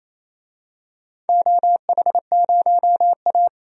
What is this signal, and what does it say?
Morse code, heard as a keyed tone, about 28 wpm, 700 Hz. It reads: OH0A
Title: DXCC-100 Call Signs - 4 Characters - 28wpm